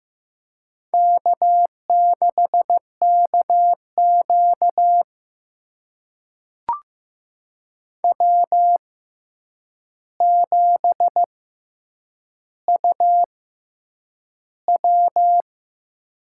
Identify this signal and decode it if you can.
Morse code, heard as a keyed tone, about 15 wpm, spaced as for 8 wpm, 700 Hz. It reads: K6KQ W7UW
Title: USA Call Signs - 4 Characters - 6x Character Spacing - 15wpm